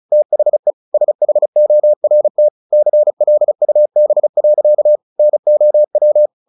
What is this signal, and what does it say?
Morse code, heard as a keyed tone, about 34 wpm, 600 Hz. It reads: THE SHORT CLUB. NOW
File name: the old man and the sea 2 35wpm